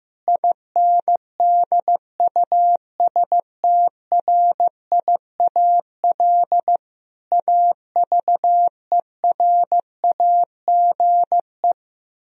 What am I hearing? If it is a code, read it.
Morse code, heard as a keyed tone, about 15 wpm, 700 Hz. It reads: INDUSTRIAL AVERAGE
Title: Speed-Racing - Sets of 5 Words 15wpm